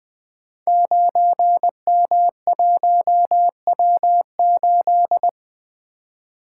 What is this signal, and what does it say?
Morse code, heard as a keyed tone, about 20 wpm, 700 Hz. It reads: 9M1W8
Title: Code Groups - 5 Letter:Numbers - 20wpm